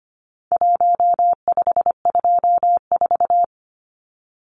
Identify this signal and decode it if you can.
Morse code, heard as a keyed tone, about 25 wpm, 700 Hz. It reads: 1524
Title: Numbers - 4-Digit 25wpm